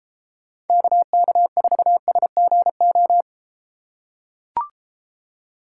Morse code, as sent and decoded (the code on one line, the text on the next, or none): -.- -.- ....- ... --. ---
KK4SGO